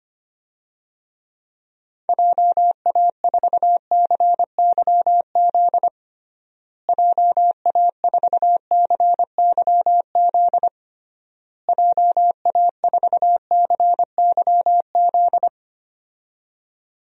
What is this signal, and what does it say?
Morse code, heard as a keyed tone, about 25 wpm, 700 Hz. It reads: JA4CYZ JA4CYZ JA4CYZ
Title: DXCC-100 Call Signs - 6 Characters - Repeated 3x - 25wpm